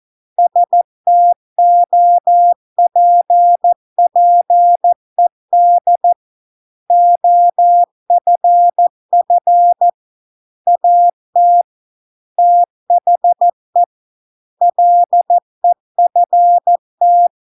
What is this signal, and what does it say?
Morse code, heard as a keyed tone, about 14 wpm, 700 Hz. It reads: STOPPED OFF AT THE LEFT